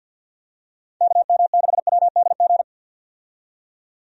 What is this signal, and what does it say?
Morse code, heard as a keyed tone, about 50 wpm, 700 Hz. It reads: KM6WDG
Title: USA Call Signs - 6 Characters - 50wpm